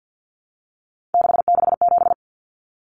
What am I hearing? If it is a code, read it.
Morse code, heard as a keyed tone, about 50 wpm, 700 Hz. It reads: 667
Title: Common Cut Numbers - 3-Digit 50wpm